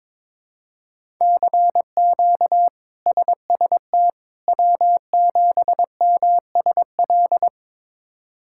Morse code, heard as a keyed tone, about 22 wpm, 700 Hz. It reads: CQ SST W7MSL
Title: K1USN SST QSO Practice - 22wpm